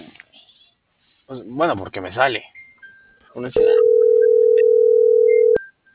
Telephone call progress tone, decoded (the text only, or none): ringback tone